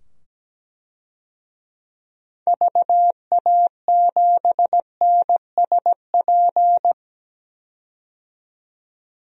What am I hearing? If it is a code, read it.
Morse code, heard as a keyed tone, about 17 wpm, 700 Hz. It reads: VA7NSP
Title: DXCC-100 Call Signs - 6 Characters - 17wpm